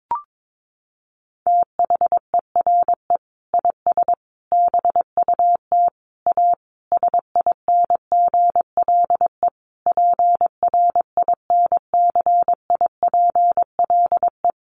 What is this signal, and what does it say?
Morse code, heard as a keyed tone, about 22 wpm, 700 Hz. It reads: THERE IS BUT A SINGLE PRINCIPLE
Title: Sentences from Top 1400 Words 22wpm